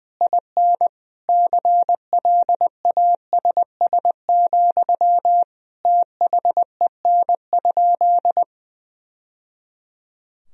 Morse code, heard as a keyed tone, about 20 wpm, 700 Hz. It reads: IN CLASS, THEN?
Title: Sentences from Top 1000 Words - Review Part 4 - 20wpm